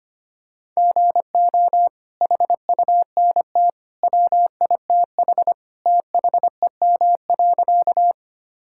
Morse code, heard as a keyed tone, about 25 wpm, 700 Hz. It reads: GO HUNT WITH THEM.
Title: Sentences from Top 1600 Words 25wpm